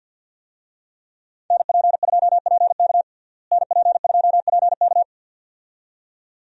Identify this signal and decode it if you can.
Morse code, heard as a keyed tone, about 47 wpm, 700 Hz. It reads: NP2PK NP2PK
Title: USA Call Signs - 5 Characters - Repeated 2x - 50wpm